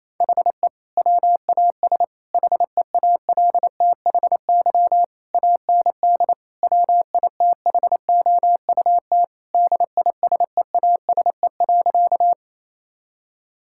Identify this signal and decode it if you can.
Morse code, heard as a keyed tone, about 28 wpm, 700 Hz. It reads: HE WAS HEALTHY AND WITHOUT DISEASE.
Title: Sentences from Top 1000 Words - Review Part 6 - 28wpm